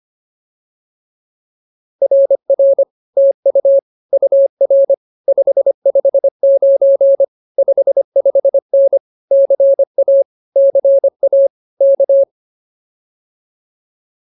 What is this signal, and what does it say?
Morse code, heard as a keyed tone, about 24 wpm, 550 Hz. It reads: RR TU UR 559 55N CA CA K